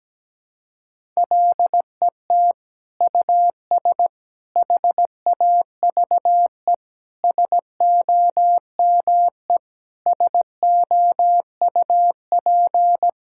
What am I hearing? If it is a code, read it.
Morse code, heard as a keyed tone, about 17 wpm, 700 Hz. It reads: LET US HAVE SOME SOUP
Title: Sentences from Top 1000 Words - Review Part 6 - 17wpm